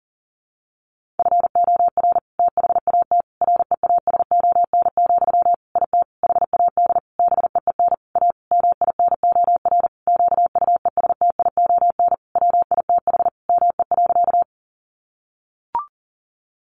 Morse code, heard as a keyed tone, about 40 wpm, 700 Hz. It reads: FOR THAT REASON, IT HAD BEEN A MINOR QUESTION WITH ME.
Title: Sentences from Top 1500 Words 40wpm